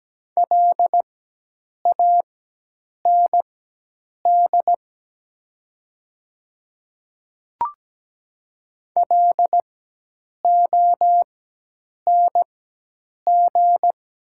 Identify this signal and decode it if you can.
Morse code, heard as a keyed tone, about 17 wpm, spaced as for 8 wpm, 700 Hz. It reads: LAND LONG
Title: 4x Character Spacing - Top 500 Words - 17wpm